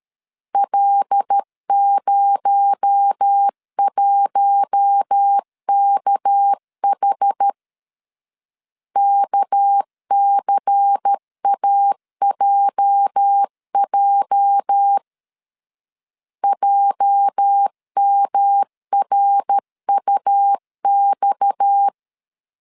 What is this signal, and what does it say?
Morse code, heard as a keyed tone, about 13 wpm, 800 Hz. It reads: L01KH KCAJJ JMRUX
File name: LEZIONI / 8 YJ45 / 8 YJ 12wpm 2235